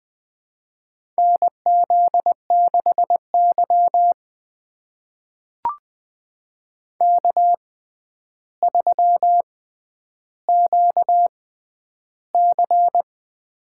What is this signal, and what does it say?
Morse code, heard as a keyed tone, about 20 wpm, spaced as for 10 wpm, 700 Hz. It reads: NZ6Y K3QC
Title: USA Call Signs - 4 Characters - 6x Character Spacing - 20wpm